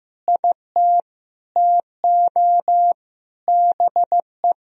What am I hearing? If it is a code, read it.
Morse code, heard as a keyed tone, about 16 wpm, 700 Hz. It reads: IT TO BE
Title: Sentences from Top 300 Words and Abbreviations 15wpm